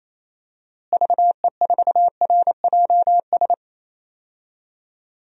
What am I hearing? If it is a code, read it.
Morse code, heard as a keyed tone, about 28 wpm, 700 Hz. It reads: VE4RJS
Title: DXCC-100 Call Signs - 6 Characters - Repeated 3x - 28wpm